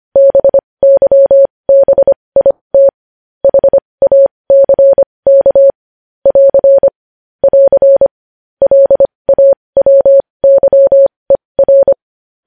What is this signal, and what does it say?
Morse code, heard as a keyed tone, about 25 wpm, spaced as for 20 wpm, 550 Hz. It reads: BYBIT HACK <AR> <AR> LAWYER
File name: podcast / News headlines-1x-25x20